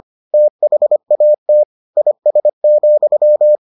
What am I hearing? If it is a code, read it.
Morse code, heard as a keyed tone, about 25 wpm, 600 Hz. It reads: THAT IS,